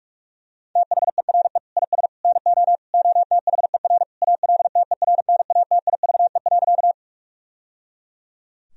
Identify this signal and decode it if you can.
Morse code, heard as a keyed tone, about 42 wpm, 700 Hz. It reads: THERE IS NO OTHER ALTERNATIVE.
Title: Sentences from Top 1000 Words - Review Part 4 - 45wpm